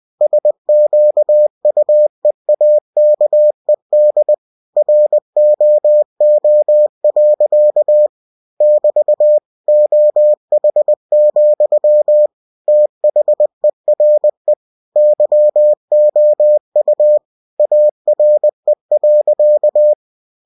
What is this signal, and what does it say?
Morse code, heard as a keyed tone, about 20 wpm, 600 Hz. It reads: SQUEAKED ROO. <BT> OH, THERE YOU ARE.
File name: cw-books / winnie-the-pooh / winnie-20wpm-page-0088